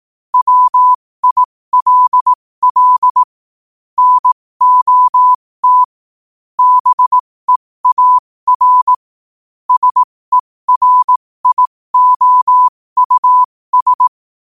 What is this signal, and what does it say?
Morse code, heard as a keyed tone, about 18 wpm, spaced as for 15 wpm, 1000 Hz. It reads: WILL NOT BEAR SERIOUS
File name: qotdfiles / 15wpm